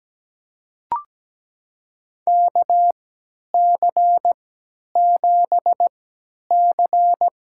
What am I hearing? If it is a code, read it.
Morse code, heard as a keyed tone, about 17 wpm, spaced as for 10 wpm, 700 Hz. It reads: KC7C
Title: USA Call Signs - 3x Character Spacing - 17wpm